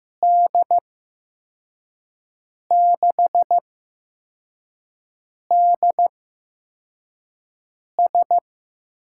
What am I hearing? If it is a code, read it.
Morse code, heard as a keyed tone, about 15 wpm, spaced as for 4 wpm, 700 Hz. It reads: D6DS
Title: USA Call Signs - 6 Characters - 8x Character Spacing - 15wpm